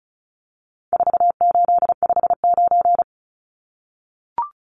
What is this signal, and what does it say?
Morse code, heard as a keyed tone, about 34 wpm, 700 Hz. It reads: 4859
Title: Numbers - 4-Digit 35wpm